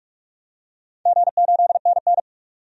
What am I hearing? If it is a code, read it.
Morse code, heard as a keyed tone, about 48 wpm, 700 Hz. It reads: G8NN
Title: DXCC-100 Call Signs - 4 Characters - 45wpm